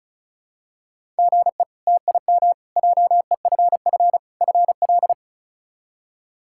Morse code, heard as a keyed tone, about 36 wpm, 700 Hz. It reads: GE TIM JEFF FL
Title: K1USN SST QSO Practice - 35wpm